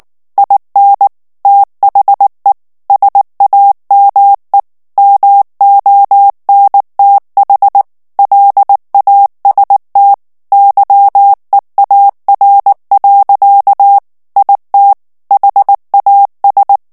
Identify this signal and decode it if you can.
Morse code, heard as a keyed tone, about 19 wpm, 800 Hz. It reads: IN THE SAME MONTH LAST YEAR. IT HAS
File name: ANGLAIS texte clair 2 vitesse 18 mots minute